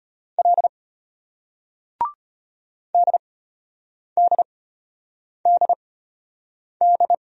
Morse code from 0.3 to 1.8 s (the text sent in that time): L